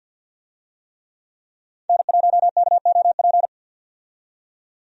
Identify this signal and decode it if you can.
Morse code, heard as a keyed tone, about 50 wpm, 700 Hz. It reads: N1KOP
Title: USA Call Signs - 5 Characters - 50wpm